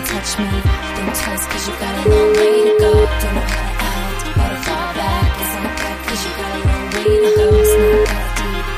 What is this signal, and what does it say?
Ringback tone (Europe, 425 Hz), with noise.